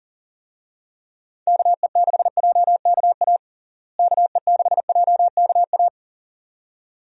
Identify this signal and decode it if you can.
Morse code, heard as a keyed tone, about 40 wpm, 700 Hz. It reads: KE6JKA KE6JKA
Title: USA Call Signs - 6 Characters - Repeated 2x - 40wpm